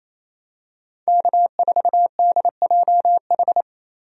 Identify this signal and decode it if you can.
Morse code, heard as a keyed tone, about 28 wpm, 700 Hz. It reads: K4DJH
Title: USA Call Signs - 5 Characters - Repeated 2x - 28wpm